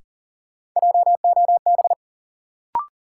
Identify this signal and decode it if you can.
Morse code, heard as a keyed tone, about 40 wpm, 700 Hz. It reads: JOB
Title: Sentences from Top 600 Words - New Word Drill - 40wpm